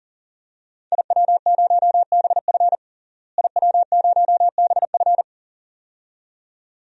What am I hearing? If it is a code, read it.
Morse code, heard as a keyed tone, about 40 wpm, 700 Hz. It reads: IW0BF IW0BF
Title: DXCC-100 Call Signs - 5 Characters - Repeated 2x - 40wpm